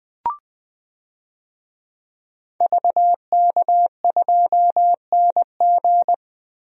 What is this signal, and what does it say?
Morse code, heard as a keyed tone, about 20 wpm, 700 Hz. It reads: VK2NG
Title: DXCC-100 Call Signs - 5 Characters - 20wpm